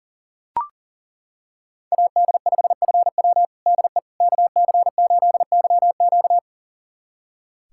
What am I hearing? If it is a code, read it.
Morse code, heard as a keyed tone, about 40 wpm, 700 Hz. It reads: AD5FW DE KC8YQ
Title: USA Callsign DE Callsign - 5 Characters - 40wpm